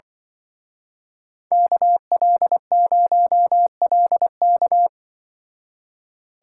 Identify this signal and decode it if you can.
Morse code, heard as a keyed tone, about 23 wpm, 700 Hz. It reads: KL0LK
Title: Speed Racing - USA 2x2 Call Signs 20wpm